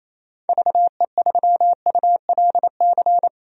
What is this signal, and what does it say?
Morse code, heard as a keyed tone, about 27 wpm, 700 Hz. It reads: VE3ULC
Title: DXCC-100 Call Signs - 6 Characters - Repeated 2x - 28wpm